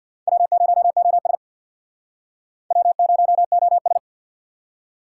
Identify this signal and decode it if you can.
Morse code, heard as a keyed tone, about 64 wpm, 700 Hz. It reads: W0OS W0OS